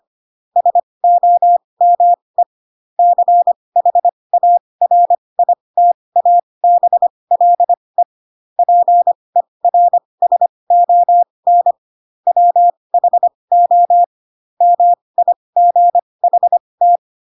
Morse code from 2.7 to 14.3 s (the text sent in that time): CHARITABLE PERSON WHO